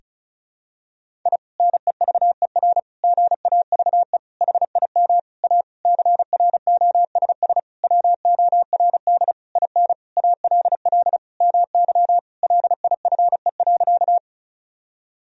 Morse code, heard as a keyed tone, about 36 wpm, 700 Hz. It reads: I NEVER GAVE HIM A CROSS WORD IN ALL MY LIFE.
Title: Sentences from Top 1000 Words - Review Part 1 - 35wpm